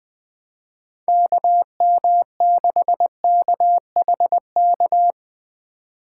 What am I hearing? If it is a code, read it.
Morse code, heard as a keyed tone, about 19 wpm, 700 Hz. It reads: KM6KHK
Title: Speed Racing - USA 2x3 Call Signs 17wpm